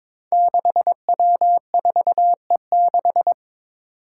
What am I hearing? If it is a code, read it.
Morse code, heard as a keyed tone, about 22 wpm, 700 Hz. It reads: 6W4E6
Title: Five Challenging Letter-Number Combinations 22wpm